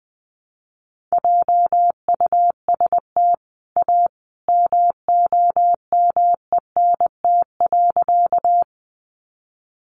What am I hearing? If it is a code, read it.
Morse code, heard as a keyed tone, about 19 wpm, 700 Hz. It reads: JUST A MOMENT.
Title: Sentences from Top 1000 Words 20wpm